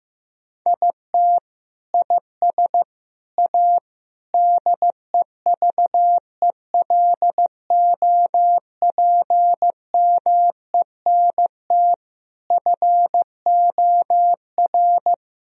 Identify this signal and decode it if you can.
Morse code, heard as a keyed tone, about 14 wpm, 700 Hz. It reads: IT IS A DEVELOPMENT FOR
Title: Sentences from Top 1000 Words - Review Part 4 - 15wpm